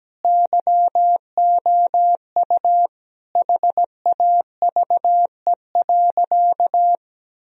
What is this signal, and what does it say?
Morse code, heard as a keyed tone, about 17 wpm, 700 Hz. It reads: YOU HAVE.
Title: Sentences from Top 1000 Words - Review Part 4 - 17wpm